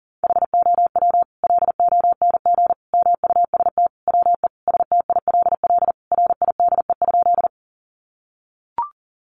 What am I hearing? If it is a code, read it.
Morse code, heard as a keyed tone, about 40 wpm, 700 Hz. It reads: HOW LONG MUST WE STILL RIDE?
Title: Sentences from Top 1200 Words 40wpm